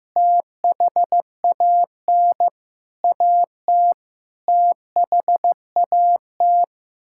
Morse code, heard as a keyed tone, about 15 wpm, 700 Hz. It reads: THAN AT THAT
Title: Sentences from Top 1000 Words - Review Part 6 - 15wpm